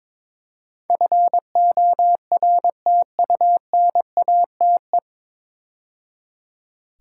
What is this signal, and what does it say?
Morse code, heard as a keyed tone, about 21 wpm, 700 Hz. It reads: FORTUNATE
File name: 9 Letter Words - First 3 Letters Given - 22wpm